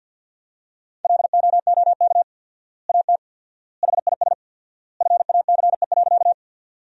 Morse code, heard as a keyed tone, about 50 wpm, 700 Hz. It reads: LOOK AT HIS FACE.
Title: 4x Word-Spacing - Sentences from Top 300 Words - 50wpm